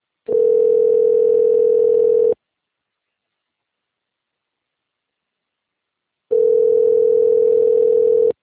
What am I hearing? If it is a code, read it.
Ringback tone (North America), 440 + 480 Hz.